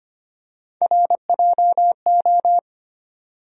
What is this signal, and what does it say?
Morse code, heard as a keyed tone, about 25 wpm, 700 Hz. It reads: RJO
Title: USA Call Sign Postfixes - 3 Letters 25wpm